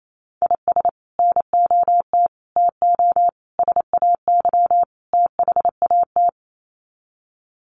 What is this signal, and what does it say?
Morse code, heard as a keyed tone, about 28 wpm, 700 Hz. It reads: IS NOT TO SAY THAT
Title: Sets of 5 Words 28wpm